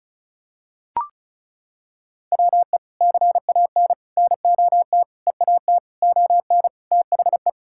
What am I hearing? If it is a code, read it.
Morse code, heard as a keyed tone, about 36 wpm, 700 Hz. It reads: WE CAN NOT EAT ON THE
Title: Sentences from Top 1000 Words - Review Part 6 - 35wpm